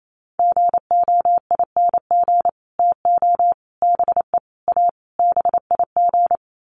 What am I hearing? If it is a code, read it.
Morse code, heard as a keyed tone, about 28 wpm, 700 Hz. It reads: GOING TO BE A BIG
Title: Sets of 5 Words Encore2 28wpm